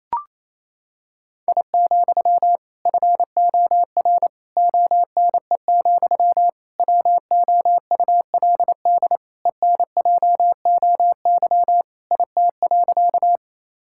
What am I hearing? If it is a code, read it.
Morse code, heard as a keyed tone, about 27 wpm, 700 Hz. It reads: I, FOR ONE, WOULD ENJOY IT.
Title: Sentences from Top 1000 Words - Review Part 5 - 28wpm